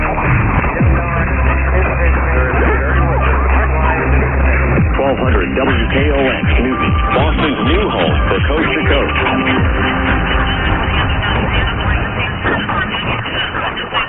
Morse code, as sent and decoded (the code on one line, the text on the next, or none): none